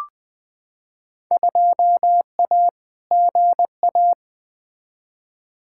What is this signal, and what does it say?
Morse code, heard as a keyed tone, about 20 wpm, 700 Hz. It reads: 2A GA
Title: Field Day Exchanges - 20wpm